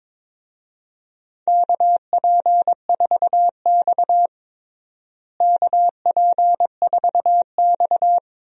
Morse code, heard as a keyed tone, about 22 wpm, 700 Hz. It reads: KP4X KP4X
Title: DXCC-100 Call Signs - 4 Characters - Repeated 3x - 22wpm